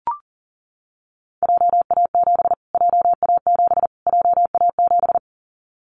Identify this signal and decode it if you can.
Morse code, heard as a keyed tone, about 40 wpm, 700 Hz. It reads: JA7 JA7 JA7
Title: Callsign Prefixes for Top 25 DXCC - Repeated 3x - 40wpm